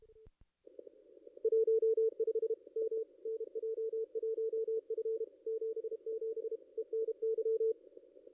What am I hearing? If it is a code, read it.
Morse code, heard as a keyed tone, about 33 wpm, 450 Hz. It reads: 15 M NJ1F 77 ENY
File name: contests / ss-cw-k5tr-op-n5rz -2200